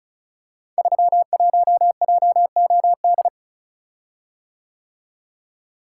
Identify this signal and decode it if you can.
Morse code, heard as a keyed tone, about 34 wpm, 700 Hz. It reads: 31JOD
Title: Code Groups - 5 Letter:Numbers - 35wpm